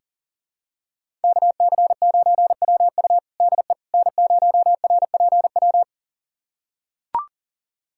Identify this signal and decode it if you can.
Morse code, heard as a keyed tone, about 37 wpm, 700 Hz. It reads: KC9WU DE N0RPW
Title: USA Callsign DE Callsign - 5 Characters - 40wpm